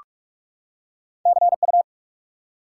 Morse code, heard as a keyed tone, about 46 wpm, 700 Hz. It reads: CU
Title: Top 90 QSO Elements 45wpm